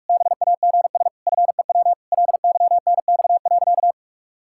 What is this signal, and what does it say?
Morse code, heard as a keyed tone, about 46 wpm, 700 Hz. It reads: BAGS FEW LYNX.
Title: Pangram Challenge v2 - 45wpm